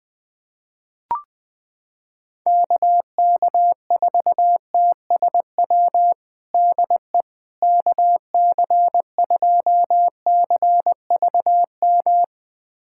Morse code, heard as a keyed tone, about 20 wpm, 700 Hz. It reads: KK4TSW DE KC2CVM
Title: USA Callsign DE Callsign - 6 Characters - 20wpm